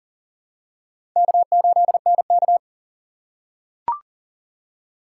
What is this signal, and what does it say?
Morse code, heard as a keyed tone, about 38 wpm, 700 Hz. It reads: K8NK
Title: USA Call Signs - 4 Characters - 8x Character Spacing - 40wpm